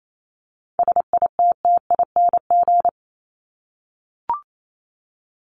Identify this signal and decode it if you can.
Morse code, heard as a keyed tone, about 28 wpm, 700 Hz. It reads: SITTING
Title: ING Suffix - Top 100 Words with Base Words 28wpm